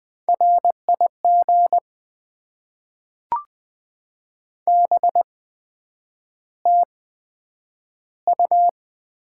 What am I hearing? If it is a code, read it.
Morse code, heard as a keyed tone, about 20 wpm, spaced as for 9 wpm, 700 Hz. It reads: RIG BTU
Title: Top 20 QSO Elements - 8x Character Spacing - 20wpm